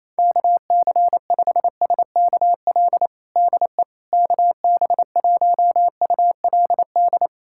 Morse code, heard as a keyed tone, about 28 wpm, 700 Hz. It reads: KC5SKL DE KB1ULD
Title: USA Callsign DE Callsign - 6 Characters - 28wpm